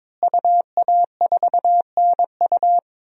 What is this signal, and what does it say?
Morse code, heard as a keyed tone, about 21 wpm, 700 Hz. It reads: UA4NU
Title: DXCC-100 Call Signs - 5 Characters - Repeated 3x - 22wpm